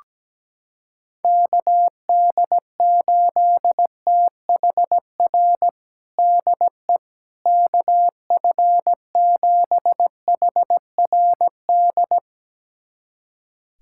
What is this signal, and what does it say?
Morse code, heard as a keyed tone, about 17 wpm, 700 Hz. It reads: KD8THR DE KF7HRD
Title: USA Callsign DE Callsign - 6 Characters - 17wpm